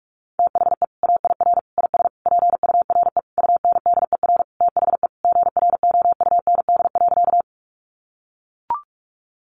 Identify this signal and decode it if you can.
Morse code, heard as a keyed tone, about 44 wpm, 700 Hz. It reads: THE AIR IS PURE UNDER THE GROUND.
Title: Sentences from Top 1500 Words 45wpm